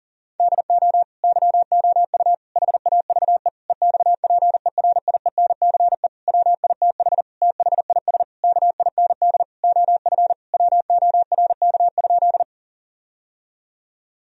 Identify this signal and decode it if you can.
Morse code, heard as a keyed tone, about 40 wpm, 700 Hz. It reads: DO YOU HAVE EXPERIENCE WITH THIS KIND OF WORK?
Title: Sentences from Top 1000 Words - Review Part 3 - 40wpm